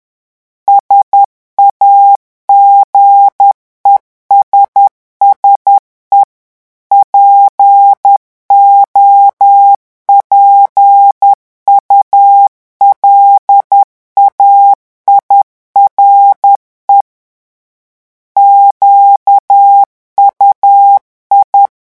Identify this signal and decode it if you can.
Morse code, heard as a keyed tone, about 10 wpm, 800 Hz. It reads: SAGESSE POPULAIRE QUI
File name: morse mp3 / revision francais 10 mots minute